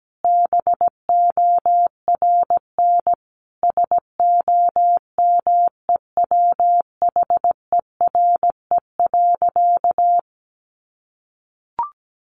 Morse code, heard as a keyed tone, about 17 wpm, 700 Hz. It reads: BORN SOMEWHERE.
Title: Sentences from Top 1400 Words 17wpm